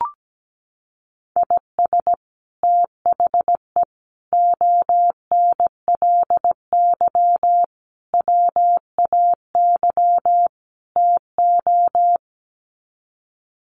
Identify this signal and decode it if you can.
Morse code, heard as a keyed tone, about 17 wpm, 700 Hz. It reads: IS THE ONLY WAY TO
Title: Sets of 5 Words Encore 17wpm